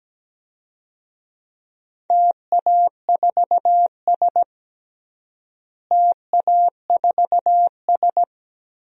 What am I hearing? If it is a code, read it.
Morse code, heard as a keyed tone, about 17 wpm, 700 Hz. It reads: TA4S TA4S
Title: DXCC-100 Call Signs - 4 Characters - Repeated 3x - 17wpm